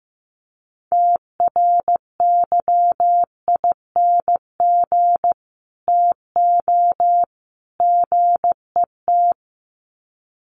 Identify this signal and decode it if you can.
Morse code, heard as a keyed tone, about 15 wpm, 700 Hz. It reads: TRYING TO GET
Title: Sets of 3 Words 15wpm